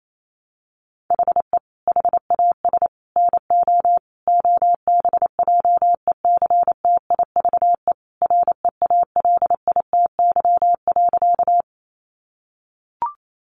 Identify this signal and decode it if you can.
Morse code, heard as a keyed tone, about 28 wpm, 700 Hz. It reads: HE HAS NO OBJECTIVE REALITY.
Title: Sentences from Top 1200 Words 28wpm